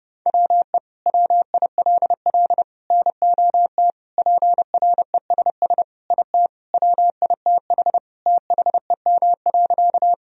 Morse code, heard as a keyed tone, about 29 wpm, 700 Hz. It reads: WE WILL NOT PRESS IT WITH THEM.
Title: Sentences from Top 1000 Words - Review Part 1 - 30wpm